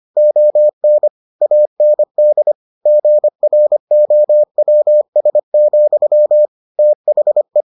Morse code, heard as a keyed tone, about 25 wpm, 600 Hz. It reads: ON AND GROWS, THE